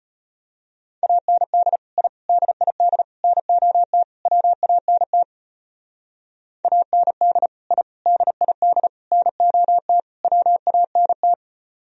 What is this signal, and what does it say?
Morse code, heard as a keyed tone, about 31 wpm, 700 Hz. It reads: AND I DID NOT WANT AND I DID NOT WANT
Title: Speed-Racing - Sets of 5 Words 25wpm